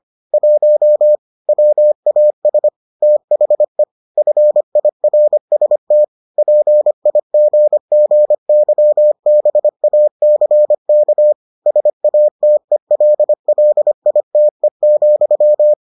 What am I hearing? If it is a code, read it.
Morse code, heard as a keyed tone, about 25 wpm, 600 Hz. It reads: 1 WAS THE FIRST PIGGYBACK SATELLITE,